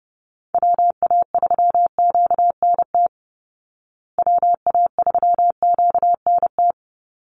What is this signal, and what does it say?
Morse code, heard as a keyed tone, about 30 wpm, 700 Hz. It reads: WA3QNT WA3QNT
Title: Call Signs Repeated 3x - 30wpm